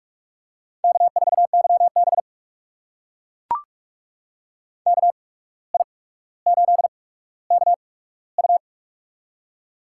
Morse code, heard as a keyed tone, about 42 wpm, spaced as for 17 wpm, 700 Hz. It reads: K4YB KI8KU
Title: USA Call Signs - 8x Character Spacing - 45wpm